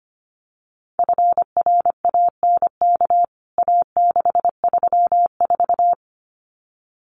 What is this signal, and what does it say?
Morse code, heard as a keyed tone, about 25 wpm, 700 Hz. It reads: FRANK A634
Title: CWT Exchange 25wpm